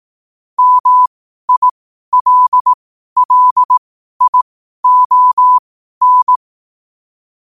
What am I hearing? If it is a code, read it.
Morse code, heard as a keyed tone, about 18 wpm, spaced as for 12 wpm, 1000 Hz. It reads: MILLION
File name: qotdfiles / 12wpm